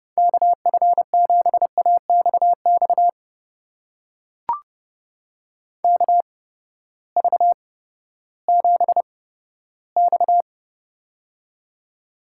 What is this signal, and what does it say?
Morse code, heard as a keyed tone, about 32 wpm, spaced as for 15 wpm, 700 Hz. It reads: KF7AXX KV7X
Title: USA Call Signs - 8x Character Spacing - 30wpm